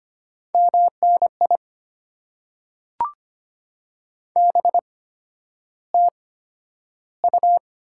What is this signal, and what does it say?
Morse code, heard as a keyed tone, about 25 wpm, spaced as for 11 wpm, 700 Hz. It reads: MNI BTU
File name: Top 90 QSO Elements - 8x Character Spacing - 25wpm